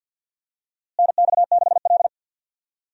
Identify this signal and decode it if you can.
Morse code, heard as a keyed tone, about 50 wpm, 700 Hz. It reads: NX6L
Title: USA Call Signs - 4 Characters - Repeated 2x - 50wpm